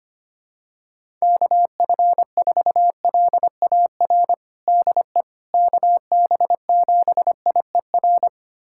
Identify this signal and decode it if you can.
Morse code, heard as a keyed tone, about 25 wpm, 700 Hz. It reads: KF4LAR DE KB7IER
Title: USA Callsign DE Callsign - 6 Characters - 25wpm